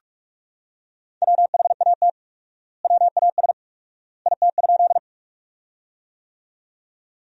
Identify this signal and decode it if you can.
Morse code, heard as a keyed tone, about 46 wpm, 700 Hz. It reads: WHAT WAS IT?
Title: 4x Word-Spacing - Sentences from Top 100 Words - 45wpm